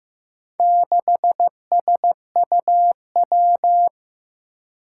Morse code, heard as a keyed tone, about 15 wpm, 700 Hz. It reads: 6SUW
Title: Four Challenging Letter-Number Combinations 15wpm